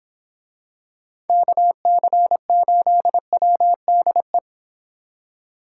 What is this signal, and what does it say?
Morse code, heard as a keyed tone, about 26 wpm, 700 Hz. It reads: KC8WDE